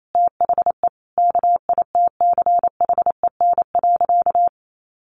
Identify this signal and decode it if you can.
Morse code, heard as a keyed tone, about 28 wpm, 700 Hz. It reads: THE KITCHEN.
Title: Sentences from Top 1400 Words 28wpm